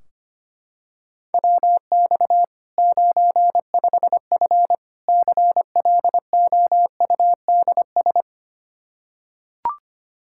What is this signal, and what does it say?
Morse code, heard as a keyed tone, about 25 wpm, 700 Hz. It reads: WX 95F CLOUDS
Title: K1USN SST QSO Practice 25wpm